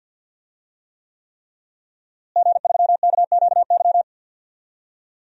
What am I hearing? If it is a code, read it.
Morse code, heard as a keyed tone, about 52 wpm, 700 Hz. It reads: G3KQY
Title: DXCC-100 Call Signs - 5 Characters - Repeated 3x - 50wpm